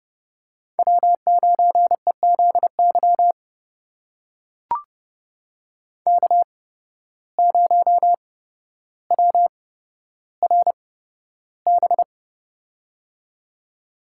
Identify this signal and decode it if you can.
Morse code, heard as a keyed tone, about 32 wpm, spaced as for 13 wpm, 700 Hz. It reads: W9EZY K0WRB
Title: USA Call Signs - 5 Characters - 8x Character Spacing - 30wpm